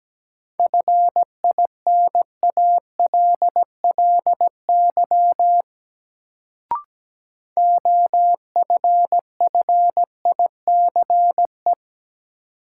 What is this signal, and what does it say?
Morse code, heard as a keyed tone, about 17 wpm, 700 Hz. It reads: FINALLY OFFICE
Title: Sentences from Top 1000 Words - New Word Drill - 17wpm